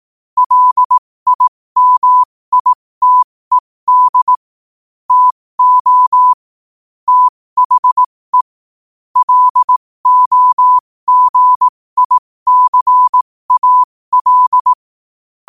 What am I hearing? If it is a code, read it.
Morse code, heard as a keyed tone, about 18 wpm, spaced as for 15 wpm, 1000 Hz. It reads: LIMITED TO THE LOGICAL
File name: qotdfiles / 15wpm